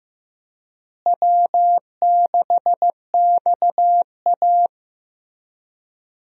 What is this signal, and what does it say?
Morse code, heard as a keyed tone, about 15 wpm, 700 Hz. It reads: W6XA
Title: USA Call Signs - 4 Characters - Repeated 3x - 15wpm